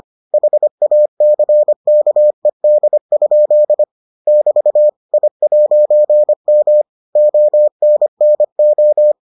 Morse code, heard as a keyed tone, about 25 wpm, 600 Hz. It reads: HACKED? <BT> I'M ONNO